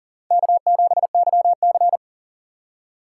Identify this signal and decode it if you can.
Morse code, heard as a keyed tone, about 40 wpm, 700 Hz. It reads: K7YC
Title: USA Call Signs - 4 Characters - 3x Character Spacing - 40wpm